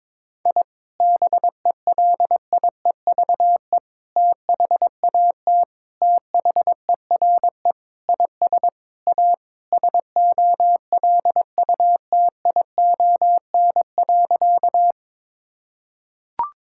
Morse code, heard as a keyed tone, about 22 wpm, 700 Hz. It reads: I BELIEVE THAT THERE IS A SOLUTION.
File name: practice / 22wpm / Sentences with 4-Word Phrases - 22wpm